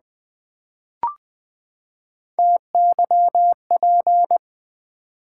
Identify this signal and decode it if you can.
Morse code, heard as a keyed tone, about 20 wpm, 700 Hz. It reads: TYP
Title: Build A Word - 3 Letters At A Time - 9-Letter Words - 20wpm